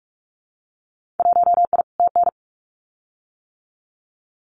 Morse code, heard as a keyed tone, about 44 wpm, 700 Hz. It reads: 1I TN
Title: Winter Field Day Exchanges 45wpm